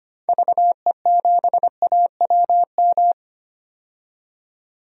Morse code, heard as a keyed tone, about 25 wpm, 700 Hz. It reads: VE7AWM
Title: DXCC-100 Call Signs - 6 Characters - 25wpm